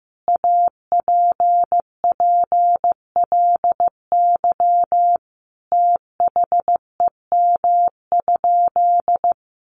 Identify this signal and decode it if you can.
Morse code, heard as a keyed tone, about 15 wpm, 700 Hz. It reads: APPLY THEM?
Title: Sentences from Top 1400 Words 15wpm